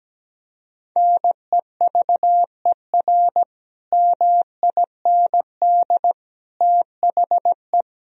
Morse code, heard as a keyed tone, about 17 wpm, 700 Hz. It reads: NEVER MIND THE
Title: Sentences from Top 1000 Words - Review Part 4 - 17wpm